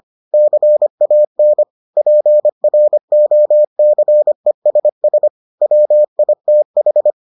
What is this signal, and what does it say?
Morse code, heard as a keyed tone, about 25 wpm, 600 Hz. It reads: CAN PROCESS WITH